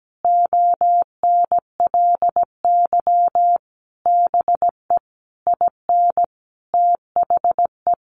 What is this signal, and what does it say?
Morse code, heard as a keyed tone, about 17 wpm, 700 Hz. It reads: ONLY BE IN THE
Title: Sentences from Top 1500 Words 17wpm